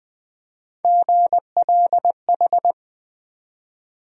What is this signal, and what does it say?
Morse code, heard as a keyed tone, about 21 wpm, 700 Hz. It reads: GLH
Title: USA Call Sign Postfixes - 3 Letters 20wpm